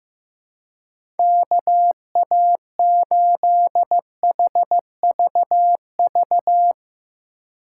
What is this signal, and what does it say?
Morse code, heard as a keyed tone, about 15 wpm, 700 Hz. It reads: KA8HVV
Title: USA Call Signs - 6 Characters - 4x Character Spacing - 15wpm